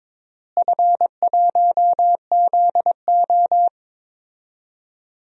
Morse code, heard as a keyed tone, about 22 wpm, 700 Hz. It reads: F1ZO
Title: DXCC-100 Call Signs - 4 Characters - Repeated 2x - 22wpm